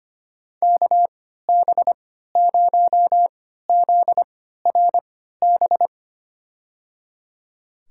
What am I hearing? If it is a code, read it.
Morse code, heard as a keyed tone, about 25 wpm, spaced as for 14 wpm, 700 Hz. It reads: KB0ZRB
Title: USA Call Signs - 6 Characters - 3x Character Spacing - 25wpm